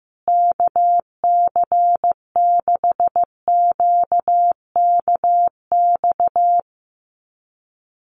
Morse code, heard as a keyed tone, about 15 wpm, 700 Hz. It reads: KC6QKX
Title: USA Call Signs Repeated 3x 15wpm